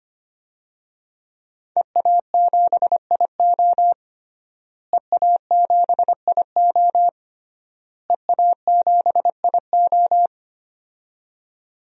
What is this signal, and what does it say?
Morse code, heard as a keyed tone, about 25 wpm, 700 Hz. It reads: EA7IO EA7IO EA7IO
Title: DXCC-100 Call Signs - 5 Characters - Repeated 3x - 25wpm